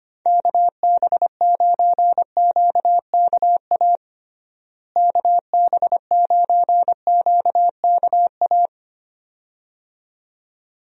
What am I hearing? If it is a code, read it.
Morse code, heard as a keyed tone, about 25 wpm, 700 Hz. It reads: KB9QKA KB9QKA
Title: USA Call Signs - 6 Characters - Repeated 2x - 25wpm